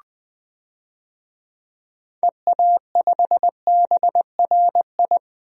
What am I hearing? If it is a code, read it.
Morse code, heard as a keyed tone, about 19 wpm, 700 Hz. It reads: EA5BRI
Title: DXCC-100 Call Signs - 6 Characters - Repeated 3x - 20wpm